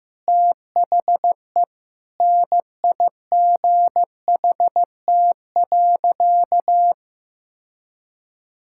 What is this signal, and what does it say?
Morse code, heard as a keyed tone, about 15 wpm, 700 Hz. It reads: THE NIGHT.
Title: Sentences from Top 1000 Words - Review Part 6 - 15wpm